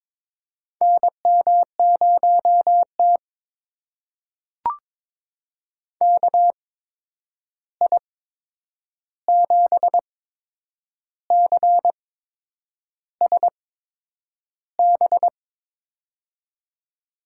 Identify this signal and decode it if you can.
Morse code, heard as a keyed tone, about 22 wpm, spaced as for 8 wpm, 700 Hz. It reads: NM0T KI7CSB
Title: USA Call Signs - 8x Character Spacing - 22wpm